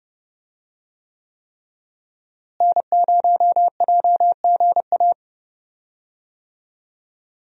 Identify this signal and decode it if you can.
Morse code, heard as a keyed tone, about 30 wpm, 700 Hz. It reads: N0JGA
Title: USA Call Signs - 5 Characters - Repeated 3x - 30wpm